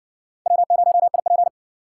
Morse code, heard as a keyed tone, about 60 wpm, 700 Hz. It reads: W0IP